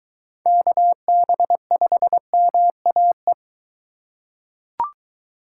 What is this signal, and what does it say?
Morse code, heard as a keyed tone, about 21 wpm, 700 Hz. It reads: KB5MAE